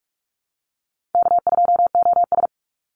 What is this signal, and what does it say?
Morse code, heard as a keyed tone, about 44 wpm, 700 Hz. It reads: K2OS
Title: Easy Call Signs - 45wpm